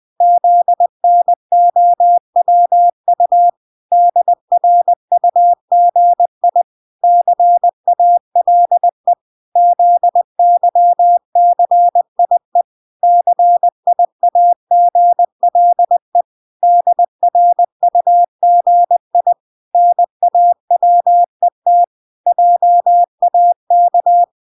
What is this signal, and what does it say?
Morse code, heard as a keyed tone, about 20 wpm, 700 Hz. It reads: ZNOWU DRUGI CALE ZYCIE CIAGLE DRUGI NAWET JAK